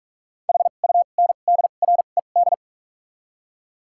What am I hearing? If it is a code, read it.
Morse code, heard as a keyed tone, about 46 wpm, spaced as for 33 wpm, 700 Hz. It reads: HUNDRED
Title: 2x Character Spacing - Top 500 Words - 45wpm